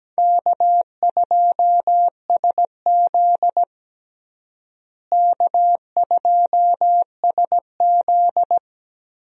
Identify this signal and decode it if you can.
Morse code, heard as a keyed tone, about 17 wpm, 700 Hz. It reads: K2SZ K2SZ
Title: USA Call Signs - 4 Characters - Repeated 2x - 17wpm